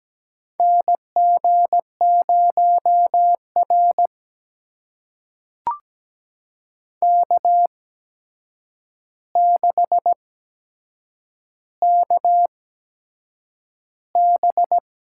Morse code, heard as a keyed tone, about 17 wpm, spaced as for 7 wpm, 700 Hz. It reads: NG0R K6KB
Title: USA Call Signs - 8x Character Spacing - 17wpm